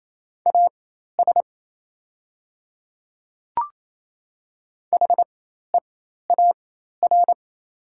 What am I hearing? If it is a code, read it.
Morse code, heard as a keyed tone, about 28 wpm, spaced as for 13 wpm, 700 Hz. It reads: AS HEAR
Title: 4x Character Spacing - Top 500 Words - 28wpm